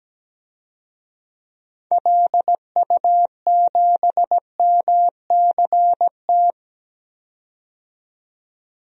Morse code, heard as a keyed tone, about 17 wpm, 700 Hz. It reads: LU7MCT
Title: DXCC-100 Call Signs - 6 Characters - 17wpm